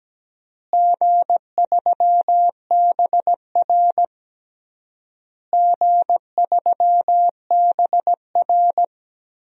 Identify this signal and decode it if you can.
Morse code, heard as a keyed tone, about 17 wpm, 700 Hz. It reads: G3BR G3BR
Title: DXCC-100 Call Signs - 4 Characters - Repeated 3x - 17wpm